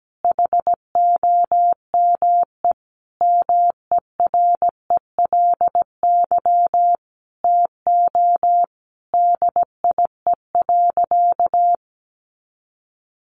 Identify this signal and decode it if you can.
Morse code, heard as a keyed tone, about 17 wpm, 700 Hz. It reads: HOME MERELY TO DIE.
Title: Sentences from Top 1400 Words 17wpm